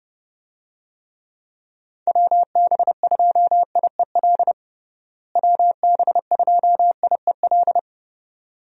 Morse code, heard as a keyed tone, about 32 wpm, 700 Hz. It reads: WB2IEL WB2IEL
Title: USA Call Signs - 6 Characters - Repeated 2x - 30wpm